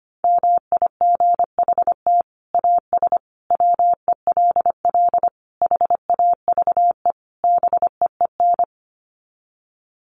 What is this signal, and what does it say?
Morse code, heard as a keyed tone, about 25 wpm, 700 Hz. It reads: MIGHT AS WELL HAVE BEEN
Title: Sets of 5 Words Encore2 25wpm